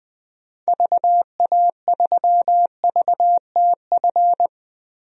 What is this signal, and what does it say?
Morse code, heard as a keyed tone, about 20 wpm, 700 Hz. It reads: VA3VTF
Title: DXCC-100 Call Signs - 6 Characters - Repeated 2x - 20wpm